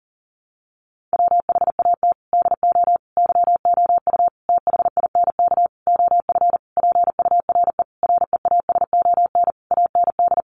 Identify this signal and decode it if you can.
Morse code, heard as a keyed tone, about 40 wpm, 700 Hz. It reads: WHAT DO YOU THINK OF PURE REASON AND
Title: Sentences from Top 1500 Words 40wpm